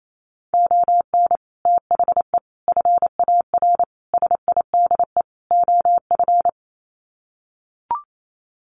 Morse code, heard as a keyed tone, about 28 wpm, 700 Hz. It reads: ON THE FAR SIDE OF
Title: Sets of 5 Words 28wpm